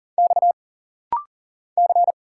Morse code, heard as a keyed tone, about 40 wpm, 700 Hz. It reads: X C
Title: Single Letters - Rapid-Fire 40wpm